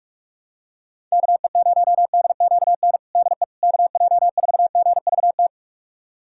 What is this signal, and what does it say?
Morse code, heard as a keyed tone, about 44 wpm, 700 Hz. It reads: KE0DQN DE KJ4GVT
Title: USA Callsign DE Callsign - 6 Characters - 45wpm